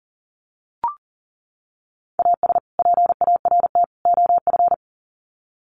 Morse code, heard as a keyed tone, about 40 wpm, 700 Hz. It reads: AS PART OF
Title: Sets of 3 Words 40wpm